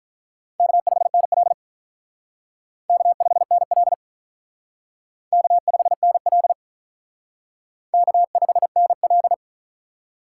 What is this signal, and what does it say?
Morse code, heard as a keyed tone, about 45 wpm, 700 Hz. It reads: K5NL K5NL K5NL K5NL